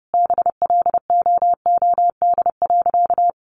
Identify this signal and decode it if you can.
Morse code, heard as a keyed tone, about 30 wpm, 700 Hz. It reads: BLOOD.
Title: Sentences from Top 1100 Words 30wpm